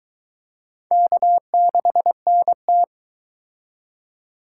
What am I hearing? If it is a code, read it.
Morse code, heard as a keyed tone, about 23 wpm, 700 Hz. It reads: K6NT